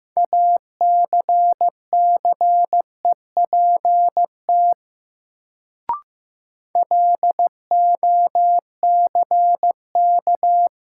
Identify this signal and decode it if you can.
Morse code, heard as a keyed tone, about 15 wpm, 700 Hz. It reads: ACCEPT LOCK
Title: Sentences from Top 1300 Words - New Word Drill - 15wpm